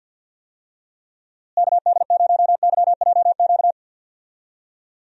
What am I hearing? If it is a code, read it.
Morse code, heard as a keyed tone, about 48 wpm, 700 Hz. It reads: KD0YJQ
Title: USA Call Signs - 6 Characters - Repeated 3x - 50wpm